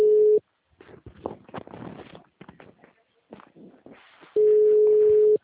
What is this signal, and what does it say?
Ringback tone (Europe, 425 Hz).